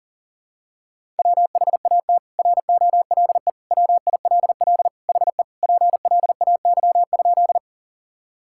Morse code, heard as a keyed tone, about 37 wpm, 700 Hz. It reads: WHAT ROLE WILL HE PLAY?
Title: Sentences from Top 1000 Words - Review Part 6 - 40wpm